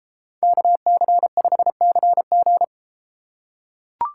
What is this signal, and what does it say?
Morse code, heard as a keyed tone, about 33 wpm, 700 Hz. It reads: KC5CG